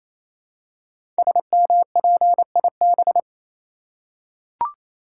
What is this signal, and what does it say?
Morse code, heard as a keyed tone, about 29 wpm, 700 Hz. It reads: SMPIB
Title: Code Groups - 5 Letters - 28wpm